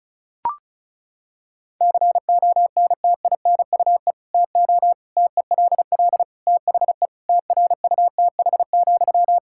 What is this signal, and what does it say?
Morse code, heard as a keyed tone, about 36 wpm, 700 Hz. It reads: CONTINUE TO TELL THE TRUTH,
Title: Sentences from Top 1000 Words - Review Part 6 - 35wpm